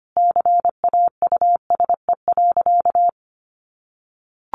Morse code, heard as a keyed tone, about 25 wpm, 700 Hz. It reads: CAUSE.
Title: Sentences from Top 700 Words 25wpm